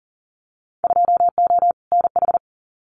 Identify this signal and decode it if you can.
Morse code, heard as a keyed tone, about 40 wpm, 700 Hz. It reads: 2O NH
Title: Winter Field Day Exchanges 40wpm